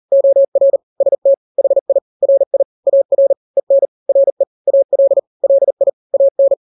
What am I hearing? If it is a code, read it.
Morse code, heard as a keyed tone, about 40 wpm, 550 Hz. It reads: OR ST HI RI AR EN RE AL LI AN
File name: n-grams / 2-grams 40 35 WPM